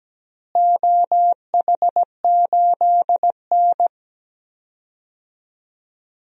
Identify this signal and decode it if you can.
Morse code, heard as a keyed tone, about 17 wpm, 700 Hz. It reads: OH8N
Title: DXCC-100 Call Signs - 4 Characters - Repeated 3x - 17wpm